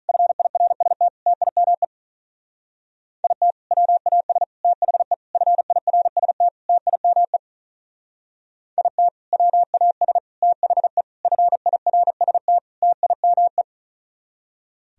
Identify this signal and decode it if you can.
Morse code, heard as a keyed tone, about 43 wpm, 700 Hz. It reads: FIRST TIME IT WAS THE FIRST TIME IT WAS THE FIRST TIME
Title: Speed-Racing - Sets of 5 Words 35wpm